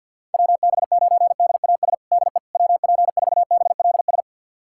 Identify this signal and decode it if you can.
Morse code, heard as a keyed tone, about 51 wpm, 700 Hz. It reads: WB9DAS DE WP4BLS
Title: USA Callsign DE Callsign - 6 Characters - 50wpm